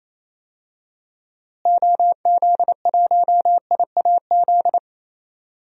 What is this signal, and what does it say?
Morse code, heard as a keyed tone, about 27 wpm, 700 Hz. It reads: OZ1IAZ
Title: DXCC-100 Call Signs - 6 Characters - Repeated 2x - 28wpm